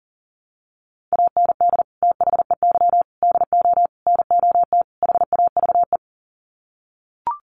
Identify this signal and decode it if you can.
Morse code, heard as a keyed tone, about 40 wpm, 700 Hz. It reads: AND THEY DO NOT HAVE
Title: Sets of 5 Words Encore2 40wpm